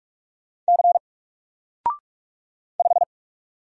Morse code, heard as a keyed tone, about 45 wpm, 700 Hz. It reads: C 5
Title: Single Letter-Number Alt 45wpm